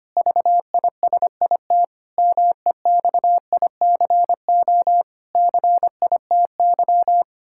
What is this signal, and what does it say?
Morse code, heard as a keyed tone, about 25 wpm, 700 Hz. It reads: VISIT MEXICO CITY
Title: Pangram Challenge v2 - 25wpm